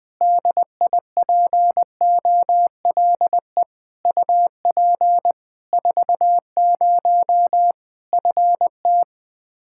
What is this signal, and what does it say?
Morse code, heard as a keyed tone, about 21 wpm, 700 Hz. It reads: DIPOLE UP 40 FT
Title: K1USN SST QSO Practice 20wpm